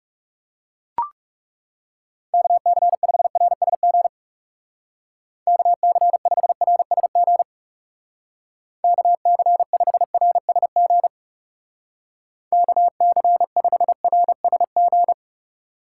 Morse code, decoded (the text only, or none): KC5RSG KC5RSG KC5RSG KC5RSG